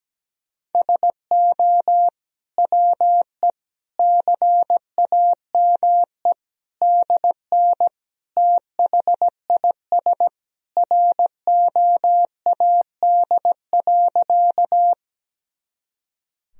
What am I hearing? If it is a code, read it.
Morse code, heard as a keyed tone, about 17 wpm, 700 Hz. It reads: SO WE CAME DN THIS ROAD.
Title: Sentences from Top 400 Words and Abbreviations 17wpm